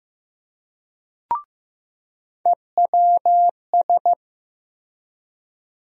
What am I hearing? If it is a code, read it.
Morse code, heard as a keyed tone, about 15 wpm, 700 Hz. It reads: EWS
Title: USA Call Sign Postfixes - 3 Letters 15wpm